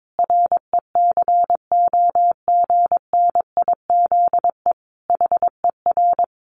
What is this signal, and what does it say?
Morse code, heard as a keyed tone, about 22 wpm, 700 Hz. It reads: RECOGNIZE HER
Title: Sentences from Top 1400 Words 22wpm